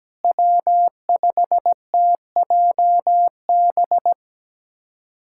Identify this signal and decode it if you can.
Morse code, heard as a keyed tone, about 17 wpm, 700 Hz. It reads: W5TJB
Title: USA Callsign DE Callsign - 5 Characters - 17wpm